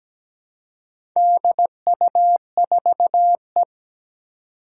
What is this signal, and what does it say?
Morse code, heard as a keyed tone, about 17 wpm, 700 Hz. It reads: DU4E
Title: Four Challenging Letter-Number Combinations 17wpm